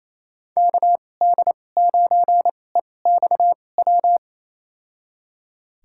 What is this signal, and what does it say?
Morse code, heard as a keyed tone, about 27 wpm, spaced as for 20 wpm, 700 Hz. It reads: KD9EXW
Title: USA Call Signs - 2x Character Spacing - 28wpm